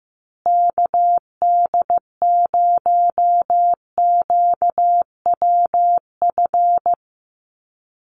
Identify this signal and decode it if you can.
Morse code, heard as a keyed tone, about 15 wpm, 700 Hz. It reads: KD0QWF
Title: USA Call Signs Repeated 3x 15wpm